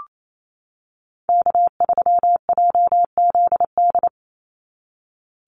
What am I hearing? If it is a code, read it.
Morse code, heard as a keyed tone, about 28 wpm, 700 Hz. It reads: K3JZD
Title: NA SOTA Chaser Callsigns 28wpm